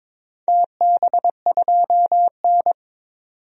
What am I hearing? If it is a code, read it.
Morse code, heard as a keyed tone, about 23 wpm, 700 Hz. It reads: TB2N
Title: DXCC-100 Call Signs - 4 Characters - 22wpm